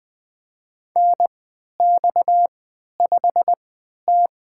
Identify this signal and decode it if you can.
Morse code, heard as a keyed tone, about 21 wpm, spaced as for 12 wpm, 700 Hz. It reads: NX5T
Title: USA Call Signs - 4 Characters - 3x Character Spacing - 20wpm